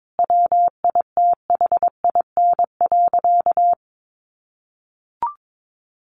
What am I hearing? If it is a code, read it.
Morse code, heard as a keyed tone, about 22 wpm, 700 Hz. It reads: WITHIN.
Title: Sentences from Top 700 Words 22wpm